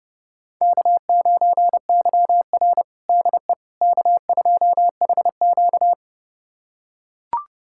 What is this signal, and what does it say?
Morse code, heard as a keyed tone, about 29 wpm, 700 Hz. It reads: K9YR DE K2HQ
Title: USA Callsign DE Callsign - 4 Characters - 30wpm